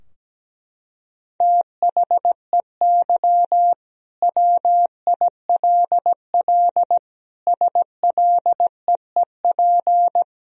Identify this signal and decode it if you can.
Morse code, heard as a keyed tone, about 17 wpm, 700 Hz. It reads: THEY WILL SLEEP
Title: Sentences from Top 1000 Words - Review Part 2 - 17wpm